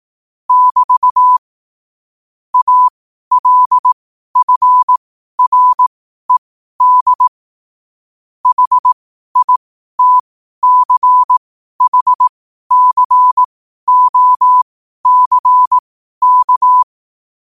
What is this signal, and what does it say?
Morse code, heard as a keyed tone, about 18 wpm, spaced as for 12 wpm, 1000 Hz. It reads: <BT> ALFRED HITCHCOCK
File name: qotdfiles / 12wpm